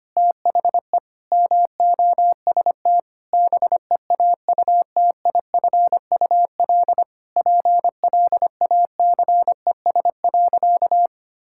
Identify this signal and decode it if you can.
Morse code, heard as a keyed tone, about 25 wpm, 700 Hz. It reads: THE MOST BEAUTIFUL PLACES.
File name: practice / 25wpm / Sentences with 4-Word Phrases - 25wpm